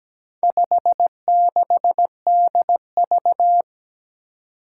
Morse code, heard as a keyed tone, about 17 wpm, 700 Hz. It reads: 56DV
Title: Four Challenging Letter-Number Combinations 17wpm